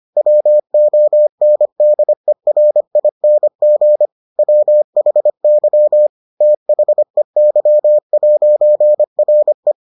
Morse code, heard as a keyed tone, about 25 wpm, 600 Hz. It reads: WONDERING WHY THEY'RE